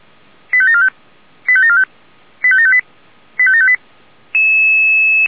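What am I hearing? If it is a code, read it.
Five-tone selective call (ZVEI), heard as quick runs of five tones, tones of about 70 ms. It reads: 85745 85745 85758 85758
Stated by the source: Five-tone ZVEI selective call with code 85727, received by SDR.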